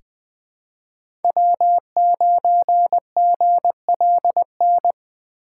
Morse code, heard as a keyed tone, about 20 wpm, 700 Hz. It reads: W9GLN
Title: USA Call Signs - 5 Characters - 4x Character Spacing - 20wpm